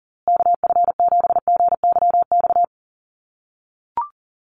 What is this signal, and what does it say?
Morse code, heard as a keyed tone, about 40 wpm, 700 Hz. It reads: KF7GYX
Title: USA Call Signs Repeated 3x 40wpm